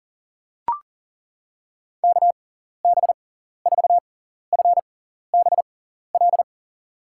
Morse code, heard as a keyed tone, about 37 wpm, spaced as for 14 wpm, 700 Hz. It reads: KB4FBL
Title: USA Call Signs - 6 Characters - 6x Character Spacing - 40wpm